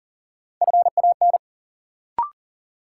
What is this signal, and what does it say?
Morse code, heard as a keyed tone, about 40 wpm, 700 Hz. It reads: FAN
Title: Sentences from Top 1300 Words - New Word Drill Alt - 40wpm